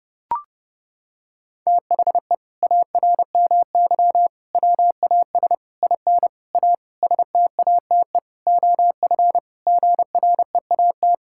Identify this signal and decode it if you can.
Morse code, heard as a keyed tone, about 30 wpm, 700 Hz. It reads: THE ARMY WAS IN A STATE OF GREAT
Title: Sentences from Top 1600 Words 30wpm